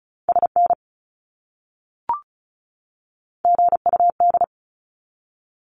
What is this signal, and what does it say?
Morse code, heard as a keyed tone, about 34 wpm, 700 Hz. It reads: SN GUD
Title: Abbreviations 35wpm